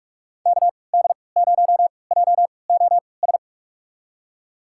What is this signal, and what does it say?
Morse code, heard as a keyed tone, about 44 wpm, spaced as for 25 wpm, 700 Hz. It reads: KD0JOS
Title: USA Call Signs - 6 Characters - 3x Character Spacing - 45wpm